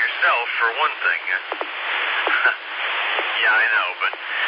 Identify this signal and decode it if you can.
Emergency Alert System (SAME) tones, heard as harsh buzzing bursts, with speech. It is heard from 1.0 to 3.8 s.